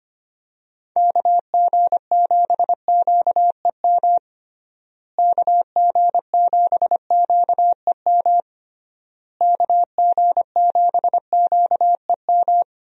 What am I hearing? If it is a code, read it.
Morse code, heard as a keyed tone, about 25 wpm, 700 Hz. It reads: KG7QEM KG7QEM KG7QEM
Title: USA Call Signs - 6 Characters - Repeated 3x - 25wpm